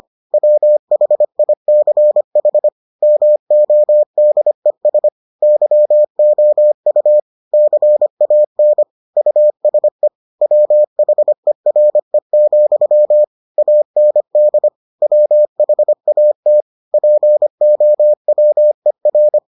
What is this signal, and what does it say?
Morse code, heard as a keyed tone, about 25 wpm, 600 Hz. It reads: WHICH MODES YOU CAN USE WHERE, AND WHAT POWER